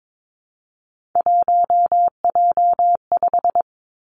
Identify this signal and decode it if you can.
Morse code, heard as a keyed tone, about 22 wpm, 700 Hz. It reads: 1J5
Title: Three Challenging Letter-Number Combinations 22wpm